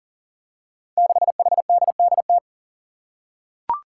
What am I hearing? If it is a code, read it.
Morse code, heard as a keyed tone, about 40 wpm, 700 Hz. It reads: 6HDDT
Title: Five Challenging Letter-Number Combinations 40wpm